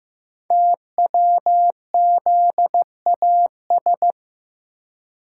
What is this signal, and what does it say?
Morse code, heard as a keyed tone, about 15 wpm, 700 Hz. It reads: TWZAS
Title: Code Groups - 5 Letters - 15wpm